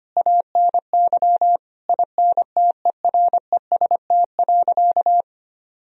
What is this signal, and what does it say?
Morse code, heard as a keyed tone, about 25 wpm, 700 Hz. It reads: ANY INTEREST.
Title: Sentences from Top 1000 Words - Review Part 2 - 25wpm